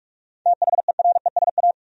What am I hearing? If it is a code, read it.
Morse code, heard as a keyed tone, about 46 wpm, 700 Hz. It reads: THERESA
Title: USA Common Names - 45wpm